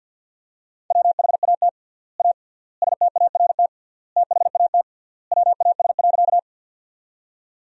Morse code, heard as a keyed tone, about 50 wpm, 700 Hz. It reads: WHAT A START THAT WAS.
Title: 3x Word-Spacing - Sentences from Top 300 Words - 50wpm